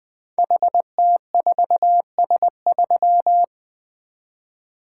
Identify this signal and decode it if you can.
Morse code, heard as a keyed tone, about 20 wpm, 700 Hz. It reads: HT4S3
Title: Code Groups - 5 Letter:Numbers - 20wpm